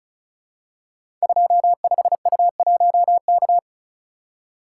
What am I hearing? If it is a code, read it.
Morse code, heard as a keyed tone, about 36 wpm, 700 Hz. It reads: 25U1K
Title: Code Groups - 5 Letter:Numbers - 35wpm